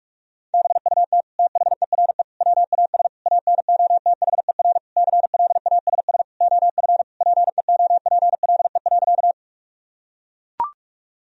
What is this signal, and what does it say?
Morse code, heard as a keyed tone, about 41 wpm, 700 Hz. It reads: BUT THERE WAS ANOTHER CLASS OF PEOPLE.
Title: Sentences from Top 1000 Words - Review Part 2 - 45wpm